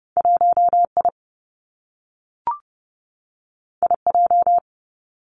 Morse code, heard as a keyed tone, about 30 wpm, 700 Hz. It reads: 1 I I J
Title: Two Challenging Letter-Number Combinations 30wpm